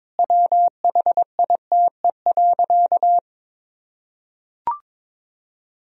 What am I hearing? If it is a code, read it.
Morse code, heard as a keyed tone, about 22 wpm, 700 Hz. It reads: WHITE.
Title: Sentences from Top 1000 Words - Review Part 1 - 22wpm